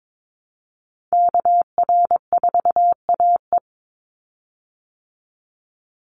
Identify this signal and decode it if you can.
Morse code, heard as a keyed tone, about 22 wpm, 700 Hz. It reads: KR4AE
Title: Call Signs Repeated 2x - 22wpm